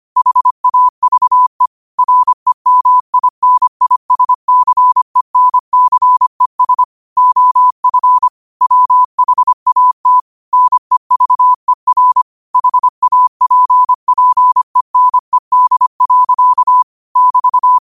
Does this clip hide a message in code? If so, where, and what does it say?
Yes: Morse code from 0.2 to 17.8 s, heard as a keyed tone, about 25 wpm, 1000 Hz. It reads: HAVE REMINISCENCES OF WHAT NEVER HAPPENED. <BT>